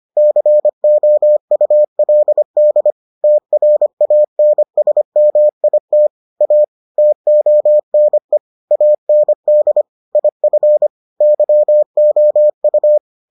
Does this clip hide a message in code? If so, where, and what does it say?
Yes: Morse code from 0.2 to 13.0 s, heard as a keyed tone, about 25 wpm, 600 Hz. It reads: COULD TRANSMIT A TONE AND IF YOU